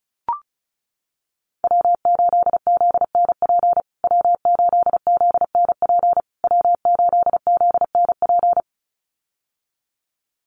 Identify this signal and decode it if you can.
Morse code, heard as a keyed tone, about 34 wpm, 700 Hz. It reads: W8ZNP W8ZNP W8ZNP
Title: USA Call Signs Repeated 3x 35wpm